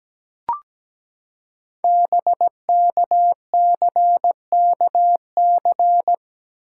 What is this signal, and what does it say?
Morse code, heard as a keyed tone, about 17 wpm, 700 Hz. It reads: BKCKC
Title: Code Groups - 5 Letters - 17wpm